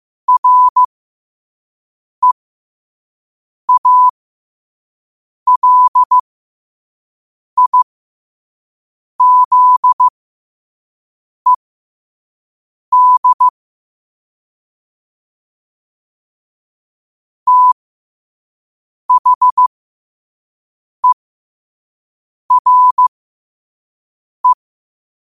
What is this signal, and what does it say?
Morse code, heard as a keyed tone, about 15 wpm, spaced as for 5 wpm, 1000 Hz. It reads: REALIZED THERE